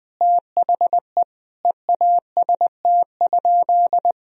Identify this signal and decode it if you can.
Morse code, heard as a keyed tone, about 20 wpm, 700 Hz. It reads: THE EAST?
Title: Sentences from Top 1000 Words - Review Part 3 - 20wpm